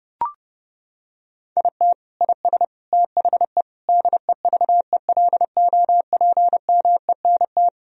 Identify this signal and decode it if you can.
Morse code, heard as a keyed tone, about 30 wpm, 700 Hz. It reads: IT IS THE DEVELOPMENT
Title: Sentences from Top 1000 Words - Review Part 4 - 30wpm